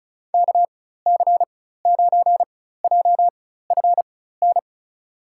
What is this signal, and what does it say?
Morse code, heard as a keyed tone, about 36 wpm, spaced as for 16 wpm, 700 Hz. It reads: KC9JFN
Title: USA Call Signs - 6 Characters - 4x Character Spacing - 35wpm